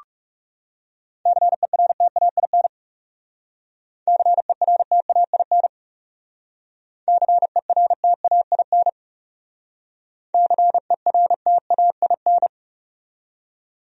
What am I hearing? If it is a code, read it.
Morse code, heard as a keyed tone, about 38 wpm, 700 Hz. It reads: CERTAIN CERTAIN CERTAIN CERTAIN
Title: Speed-Racing Top 500 Words 30wpm